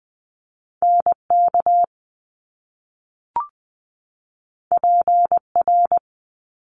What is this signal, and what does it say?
Morse code, heard as a keyed tone, about 19 wpm, 700 Hz. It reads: NK PR
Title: Top Two Letter Combinations in Words 20wpm